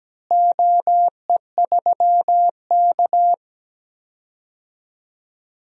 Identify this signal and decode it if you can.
Morse code, heard as a keyed tone, about 17 wpm, 700 Hz. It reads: OE3K
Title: DXCC-100 Call Signs - 4 Characters - Repeated 3x - 17wpm